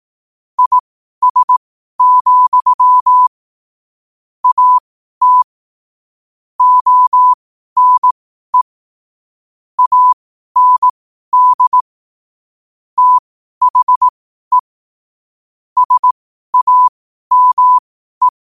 Morse code, heard as a keyed tone, about 18 wpm, spaced as for 12 wpm, 1000 Hz. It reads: IS, AT ONE AND THE SAME
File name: qotdfiles / 12wpm